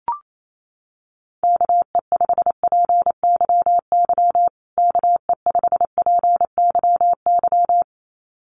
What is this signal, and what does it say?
Morse code, heard as a keyed tone, about 28 wpm, 700 Hz. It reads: KE5PYY KE5PYY
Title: USA Call Signs Repeated 2x 28wpm